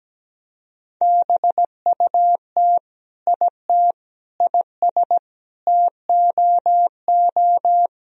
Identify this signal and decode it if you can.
Morse code, heard as a keyed tone, about 17 wpm, 700 Hz. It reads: BUT IT IS TOO
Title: Sentences from Top 1000 Words - Review Part 1 - 17wpm